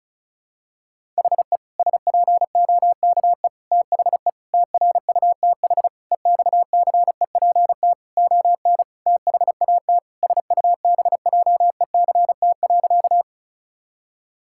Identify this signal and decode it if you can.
Morse code, heard as a keyed tone, about 34 wpm, 700 Hz. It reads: HE SPOKE THE TRUTH EXCEPT ON THAT SUBJECT.
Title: Sentences from Top 1600 Words 35wpm